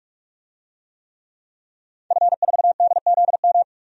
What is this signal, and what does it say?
Morse code, heard as a keyed tone, about 46 wpm, 700 Hz. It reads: F4DZM
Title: DXCC-100 Call Signs - 5 Characters - 45wpm